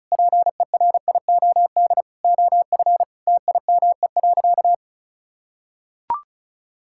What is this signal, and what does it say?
Morse code, heard as a keyed tone, about 36 wpm, 700 Hz. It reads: PERIOD OF TIME.
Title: Sentences from Top 1000 Words - Review Part 5 - 35wpm